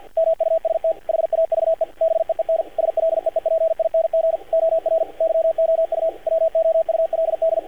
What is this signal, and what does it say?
Morse code, heard as a keyed tone, about 48 wpm, 590 Hz, with radio noise. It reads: MUST HAVE BEEN SLEEPING OR YOU WOULD
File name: archived / CRSnet